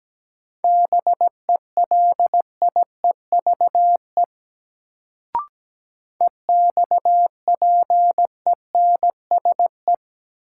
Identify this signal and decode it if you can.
Morse code, heard as a keyed tone, about 17 wpm, 700 Hz. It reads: BELIEVE EXPENSE
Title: Sets of 4 Words - New Word Drill - 17wpm